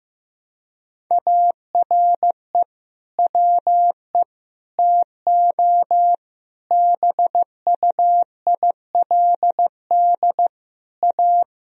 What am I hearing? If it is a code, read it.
Morse code, heard as a keyed tone, about 15 wpm, 700 Hz. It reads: ARE WE TO BUILD A
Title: Sentences from Top 1000 Words - Review Part 1 - 15wpm